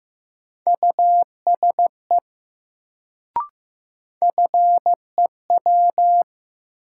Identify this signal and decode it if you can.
Morse code, heard as a keyed tone, about 15 wpm, 700 Hz. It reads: USE FEW
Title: Sets of 2 Words - New Word Drill - 15wpm